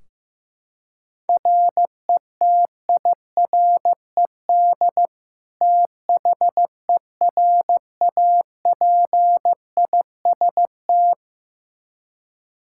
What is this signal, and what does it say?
Morse code, heard as a keyed tone, about 15 wpm, 700 Hz. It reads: RETIRED THERAPIST
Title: Practice QSO - Exchange 3 - More Info - 15wpm